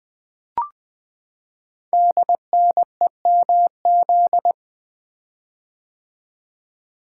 Morse code, heard as a keyed tone, about 20 wpm, 700 Hz. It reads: DNEMZ
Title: Code Groups - 5 Letters - 20wpm